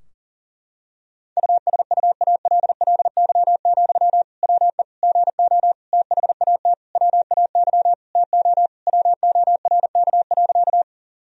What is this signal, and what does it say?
Morse code, heard as a keyed tone, about 40 wpm, 700 Hz. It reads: USUALLY, WE GO THAT WAY TO WORK.
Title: Sentences from Top 1000 Words - Review Part 3 - 40wpm